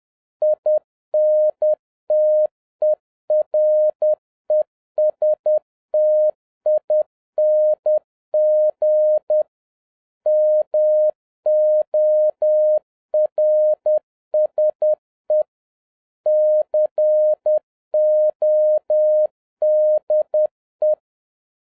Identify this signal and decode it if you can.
Morse code, heard as a keyed tone, about 10 wpm, 610 Hz. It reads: INTERESTING MORSE CODE